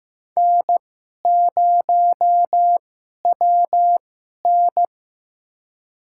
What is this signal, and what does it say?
Morse code, heard as a keyed tone, about 15 wpm, spaced as for 11 wpm, 700 Hz. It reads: N0WN
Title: USA Call Signs - 4 Characters - 2x Character Spacing - 15wpm